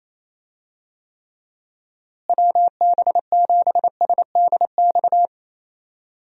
Morse code, heard as a keyed tone, about 27 wpm, 700 Hz. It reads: WB7SDX
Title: USA Call Signs - 6 Characters - Repeated 3x - 28wpm